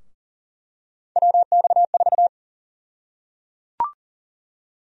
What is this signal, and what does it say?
Morse code, heard as a keyed tone, about 37 wpm, 700 Hz. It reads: WX4
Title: USA Call Sign Prefixes - Repeated 2x - 40wpm